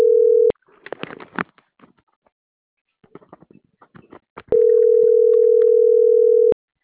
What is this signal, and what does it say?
Ringback tone (North America), 440 + 480 Hz.